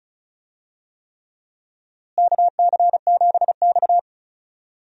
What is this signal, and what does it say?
Morse code, heard as a keyed tone, about 34 wpm, 700 Hz. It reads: KC7X
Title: USA Call Signs - 4 Characters - Repeated 3x - 35wpm